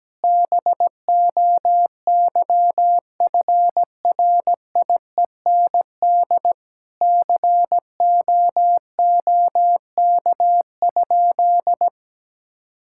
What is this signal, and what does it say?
Morse code, heard as a keyed tone, about 17 wpm, 700 Hz. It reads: BOYFRIEND COOK?
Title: Sentences from Top 1000 Words - Review Part 5 - 17wpm